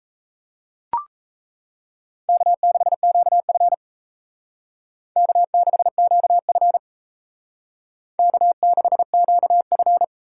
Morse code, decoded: K6QF K6QF K6QF

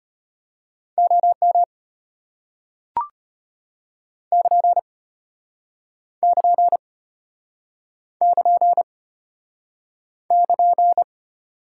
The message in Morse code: OM <KN> <KN> <KN> <KN>

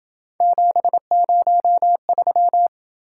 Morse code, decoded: 703

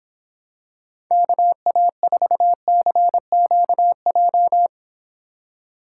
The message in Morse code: KA4CQJ

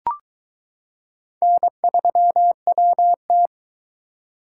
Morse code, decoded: N3WT

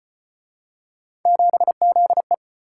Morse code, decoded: 7ZE

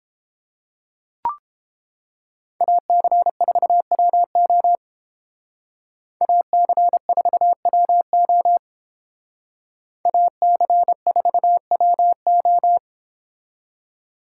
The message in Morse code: AC4WO AC4WO AC4WO

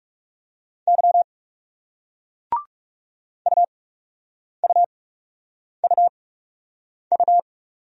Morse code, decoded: Y UUUU

